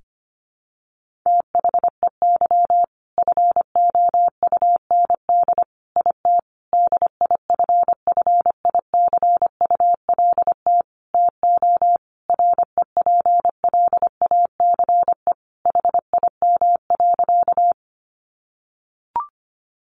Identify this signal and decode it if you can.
Morse code, heard as a keyed tone, about 25 wpm, 700 Hz. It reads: THEY FOUND IT DIFFICULT TO REPLACE HIM.
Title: Sentences from Top 1400 Words 25wpm